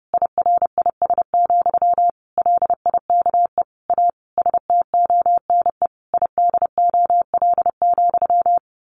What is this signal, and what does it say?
Morse code, heard as a keyed tone, about 30 wpm, 700 Hz. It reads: IRIS, LIKE A STONE IDOL,
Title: A Sound Of Thunder 30wpm